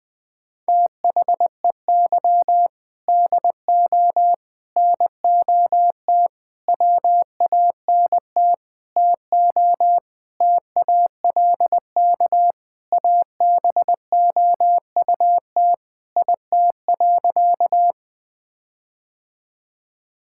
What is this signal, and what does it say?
Morse code, heard as a keyed tone, about 20 wpm, 700 Hz. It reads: THEY DO NOT WANT TO TALK ABOUT IT.
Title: Sentences with 5-Word Phrases - 20wpm